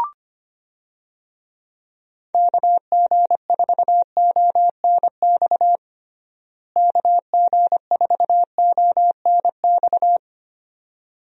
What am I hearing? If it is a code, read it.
Morse code, heard as a keyed tone, about 25 wpm, 700 Hz. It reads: KG4ONX KG4ONX
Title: USA Call Signs - 6 Characters - Repeated 2x - 25wpm